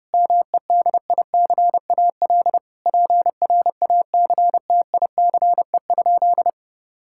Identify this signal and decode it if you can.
Morse code, heard as a keyed tone, about 30 wpm, 700 Hz. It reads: MEDICAL PRACTICE?
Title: Sentences from Top 1000 Words - Review Part 5 - 30wpm